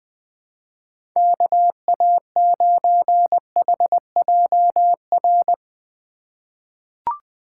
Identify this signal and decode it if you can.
Morse code, heard as a keyed tone, about 20 wpm, 700 Hz. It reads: KA9HJR
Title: USA Call Signs - 6 Characters - 2x Character Spacing - 20wpm